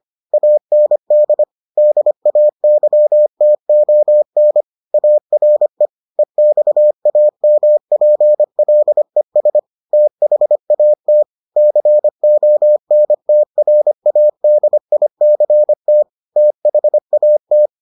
Morse code, heard as a keyed tone, about 25 wpm, 600 Hz. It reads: AND DAYTON ARE EXAMPLES THAT CONTRADICT THAT